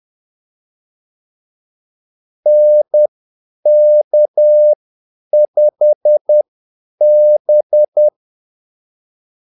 Morse code, heard as a keyed tone, about 10 wpm, spaced as for 8 wpm, 600 Hz. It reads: NK5B
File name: Callsigns-10-8wpm